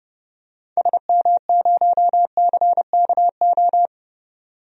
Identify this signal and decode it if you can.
Morse code, heard as a keyed tone, about 32 wpm, 700 Hz. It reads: SM0CKO
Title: DXCC-100 Call Signs - 6 Characters - 30wpm